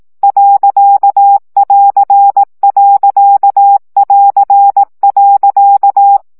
Morse code, heard as a keyed tone, about 18 wpm, 800 Hz. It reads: .<AR>.<AR>.
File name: CW gyakorlas a2 plusz pont 15 baud